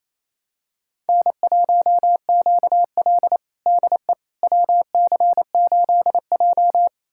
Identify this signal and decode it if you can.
Morse code, heard as a keyed tone, about 28 wpm, 700 Hz. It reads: N1QL DE WC8J
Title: USA Callsign DE Callsign - 4 Characters - 28wpm